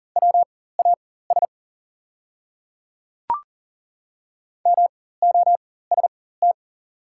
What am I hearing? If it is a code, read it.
Morse code, heard as a keyed tone, about 40 wpm, spaced as for 19 wpm, 690 Hz. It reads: WAS MOST
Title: 4x Character Spacing - Top 100 Words - 40wpm